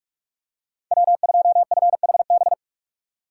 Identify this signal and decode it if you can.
Morse code, heard as a keyed tone, about 45 wpm, 690 Hz. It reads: W2FHB
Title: USA Call Signs - 5 Characters - 6x Character Spacing - 45wpm